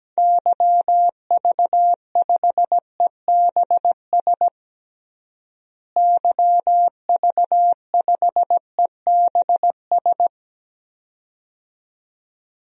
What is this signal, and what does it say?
Morse code, heard as a keyed tone, about 17 wpm, 700 Hz. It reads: YV5EBS YV5EBS
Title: DXCC-100 Call Signs - 6 Characters - Repeated 2x - 17wpm